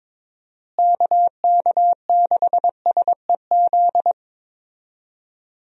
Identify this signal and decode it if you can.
Morse code, heard as a keyed tone, about 22 wpm, 700 Hz. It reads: KK6SEZ
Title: USA Call Signs - 6 Characters - Repeated 2x - 22wpm